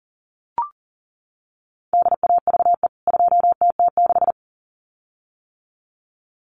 Morse code, heard as a keyed tone, about 37 wpm, 700 Hz. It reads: DAVE 2TT6
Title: CWT Exchange 40wpm